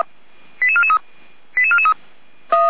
Five-tone selective call (ZVEI), heard as quick runs of five tones, tones of about 70 ms. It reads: 88443 88443